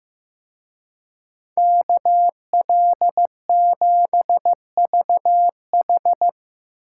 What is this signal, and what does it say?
Morse code, heard as a keyed tone, about 17 wpm, 700 Hz. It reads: KL7VH
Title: Speed Racing - USA 2x2 Call Signs 15wpm